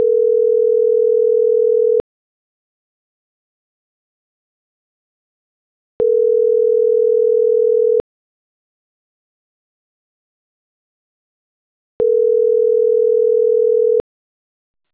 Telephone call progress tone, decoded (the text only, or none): ringback tone